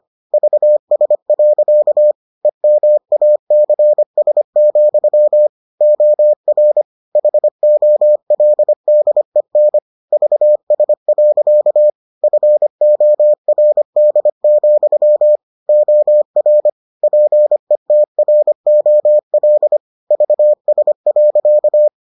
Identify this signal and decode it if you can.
Morse code, heard as a keyed tone, about 25 wpm, 600 Hz. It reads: VS. EMACS, OR HOLDEN VS. FORD, OR PETROL VS.